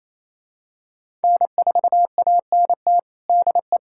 Morse code, heard as a keyed tone, about 27 wpm, 700 Hz. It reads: N4ANT DE
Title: USA Callsign DE Callsign - 5 Characters - 28wpm